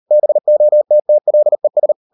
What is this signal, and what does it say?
Morse code, heard as a keyed tone, about 38 wpm, 600 Hz. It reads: BOTTLES